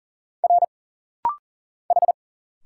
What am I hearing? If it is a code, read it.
Morse code, heard as a keyed tone, about 40 wpm, 700 Hz. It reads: R H
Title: Single Letters - Rapid-Fire 40wpm